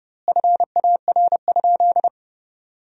Morse code, heard as a keyed tone, about 31 wpm, 700 Hz. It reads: FAR?
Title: 2x Word-Spacing - Sentences from Top 300 Words - 30wpm